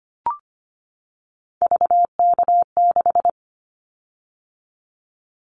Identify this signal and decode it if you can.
Morse code, heard as a keyed tone, about 25 wpm, 700 Hz. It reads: VK6
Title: Callsign Prefixes for Top 100 DXCC - 25wpm